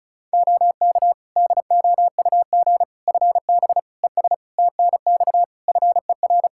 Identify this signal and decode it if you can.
Morse code, heard as a keyed tone, about 34 wpm, 700 Hz. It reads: OK DOUG FB ES TNX FER
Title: Practice QSO - Exchange 2 - Info - 35wpm